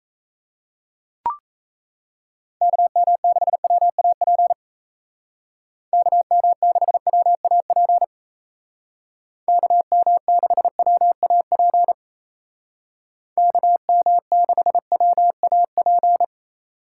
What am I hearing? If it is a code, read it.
Morse code, heard as a keyed tone, about 34 wpm, 700 Hz. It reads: KM6WAP KM6WAP KM6WAP KM6WAP
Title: Speed Racing - USA Call Signs 28wpm